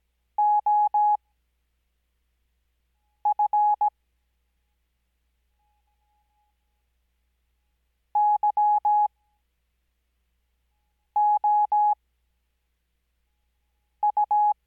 Morse code, heard as a keyed tone, about 17 wpm, spaced as for 4 wpm, 840 Hz. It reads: OF YOU
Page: Farnsworth Spacing: 15WPM Character speed Spaced at 5WPM.